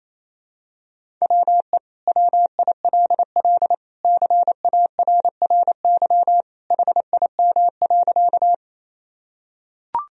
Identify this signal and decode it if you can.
Morse code, heard as a keyed tone, about 28 wpm, 700 Hz. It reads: WE WILL CARRY HIM.
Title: Sentences from Top 1000 Words - Review Part 1 - 28wpm